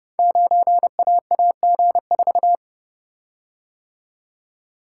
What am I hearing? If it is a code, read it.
Morse code, heard as a keyed tone, about 30 wpm, 700 Hz. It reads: 9AAG4
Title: Code Groups - 5 Letter:Numbers - 30wpm